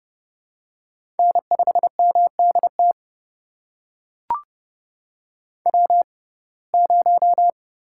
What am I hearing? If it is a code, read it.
Morse code, heard as a keyed tone, about 30 wpm, spaced as for 22 wpm, 700 Hz. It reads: N5MDT W0
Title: USA Call Signs - 6x Character Spacing - 30wpm